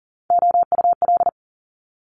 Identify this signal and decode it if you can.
Morse code, heard as a keyed tone, about 38 wpm, 700 Hz. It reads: OUL
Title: Top Three Letter Combinations in Words 40wpm